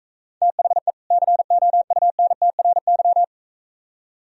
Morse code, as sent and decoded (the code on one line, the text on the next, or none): - .... / -.-. --- ..- -. - .-. -.--
THE COUNTRY